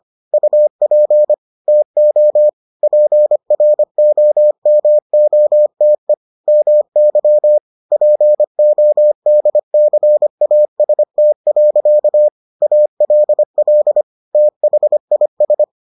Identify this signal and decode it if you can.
Morse code, heard as a keyed tone, about 25 wpm, 600 Hz. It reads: UP TO PROMOTE MY PODCAST. ALL THIS